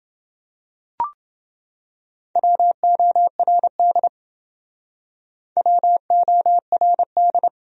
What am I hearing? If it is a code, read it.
Morse code, heard as a keyed tone, about 28 wpm, 700 Hz. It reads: WORD WORD